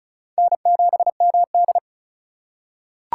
Morse code, heard as a keyed tone, about 36 wpm, 700 Hz. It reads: N7MD
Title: USA Call Signs - 4 Characters - 4x Character Spacing - 35wpm